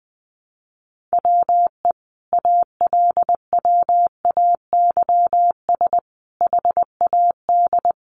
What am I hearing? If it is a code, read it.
Morse code, heard as a keyed tone, about 20 wpm, 700 Hz. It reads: WE ALWAYS HAD
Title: Sentences from Top 1300 Words 20wpm